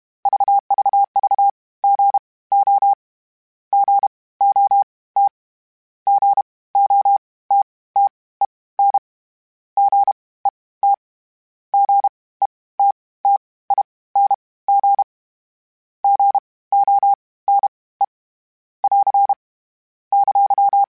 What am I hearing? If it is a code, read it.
Morse code, heard as a keyed tone, about 32 wpm, spaced as for 19 wpm, 800 Hz. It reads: VVVGO GOT GOTTEN GET GETTING GONE <AR> !
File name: Short Runs / Short Runs-mp3s / Gotten @30